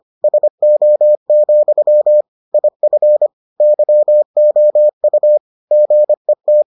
Morse code, heard as a keyed tone, about 25 wpm, 600 Hz. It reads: SO, IF YOU GET